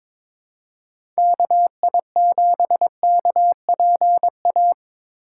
Morse code, heard as a keyed tone, about 22 wpm, 700 Hz. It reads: KI7KPA
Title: USA Call Signs - 6 Characters - 4x Character Spacing - 22wpm